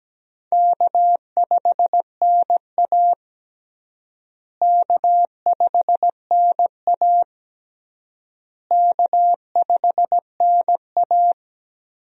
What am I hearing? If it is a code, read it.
Morse code, heard as a keyed tone, about 17 wpm, 700 Hz. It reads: K5NA K5NA K5NA
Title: USA Call Signs - 4 Characters - Repeated 3x - 17wpm